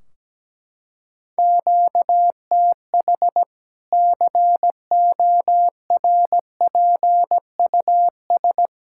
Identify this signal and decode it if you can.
Morse code, heard as a keyed tone, about 17 wpm, 700 Hz. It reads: QTH CORPUS
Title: Practice QSO - Exchange 1 - Beginning - 17wpm